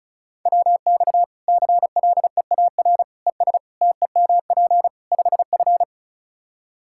Morse code, heard as a keyed tone, about 36 wpm, 700 Hz. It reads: WX CLEAR ES TEMP 5F
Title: Practice QSO - Exchange 2 - Info - 35wpm